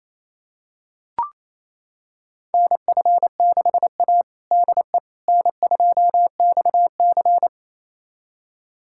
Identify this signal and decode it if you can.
Morse code, heard as a keyed tone, about 28 wpm, 700 Hz. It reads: NF6A DE N2XC
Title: USA Callsign DE Callsign - 4 Characters - 28wpm